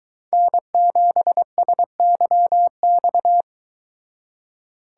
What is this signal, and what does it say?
Morse code, heard as a keyed tone, about 22 wpm, 700 Hz. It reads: N7SYX